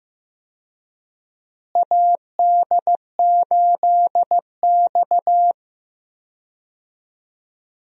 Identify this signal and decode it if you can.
Morse code, heard as a keyed tone, about 15 wpm, 700 Hz. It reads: AD8X
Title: USA Call Signs - 4 Characters - Repeated 3x - 15wpm